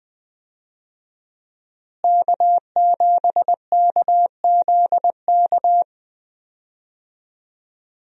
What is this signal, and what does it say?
Morse code, heard as a keyed tone, about 20 wpm, 700 Hz. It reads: K7KZK
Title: USA Call Signs - 5 Characters - Repeated 2x - 20wpm